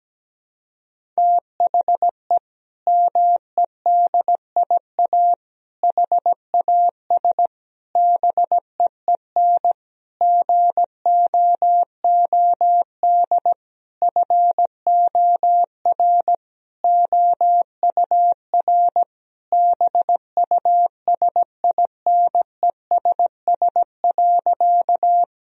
Morse code, heard as a keyed tone, about 17 wpm, 700 Hz. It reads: THE MEDIA HAS BEEN GOOD FOR OUR BUSINESS.
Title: Sentences from Top 1000 Words - Review Part 4 - 17wpm